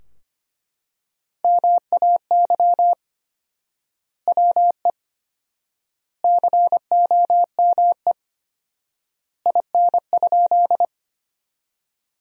Morse code, heard as a keyed tone, about 25 wpm, 700 Hz. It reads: MAY WE COME IN?
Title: 4x Word-Spacing - Sentences from Top 100 Words - 25wpm